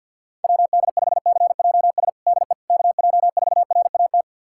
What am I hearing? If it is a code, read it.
Morse code, heard as a keyed tone, about 51 wpm, 700 Hz. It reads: WD5CJS DE KJ4RAT
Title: USA Callsign DE Callsign 50wpm